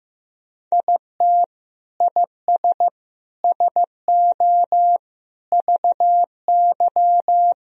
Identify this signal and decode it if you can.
Morse code, heard as a keyed tone, about 15 wpm, 700 Hz. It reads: IT IS SO VY
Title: Sentences from Top 300 Words and Abbreviations 15wpm